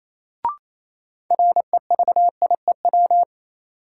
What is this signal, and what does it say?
Morse code, heard as a keyed tone, about 28 wpm, 700 Hz. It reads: REVIEW
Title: Sentences from Top 900 Words - New Word Drill - 28wpm